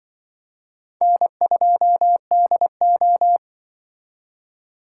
Morse code, heard as a keyed tone, about 24 wpm, 700 Hz. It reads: N2DO
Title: Speed Racing - USA 1x2 Call Signs 20wpm